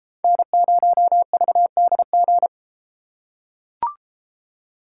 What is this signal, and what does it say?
Morse code, heard as a keyed tone, about 32 wpm, 700 Hz. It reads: N0VDG